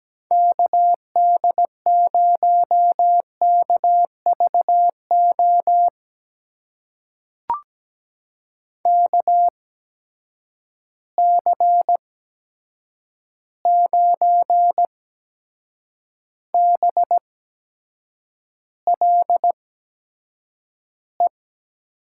Morse code, heard as a keyed tone, about 17 wpm, spaced as for 7 wpm, 700 Hz. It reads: KD0KVO KC9BLE
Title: USA Call Signs - 8x Character Spacing - 17wpm